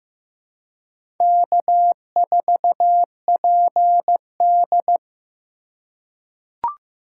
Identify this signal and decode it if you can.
Morse code, heard as a keyed tone, about 15 wpm, 700 Hz. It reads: K4PD
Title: USA Call Signs - 4 Characters - 3x Character Spacing - 15wpm